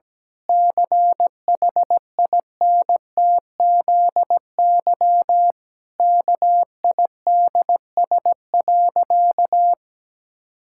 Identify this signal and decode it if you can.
Morse code, heard as a keyed tone, about 17 wpm, 700 Hz. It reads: CHINTZY KIDS.
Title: Pangram Challenge v2 - 17wpm